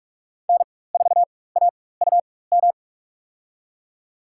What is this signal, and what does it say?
Morse code, heard as a keyed tone, about 44 wpm, spaced as for 21 wpm, 690 Hz. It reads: N4AUM
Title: USA Call Signs - 4x Character Spacing - 45wpm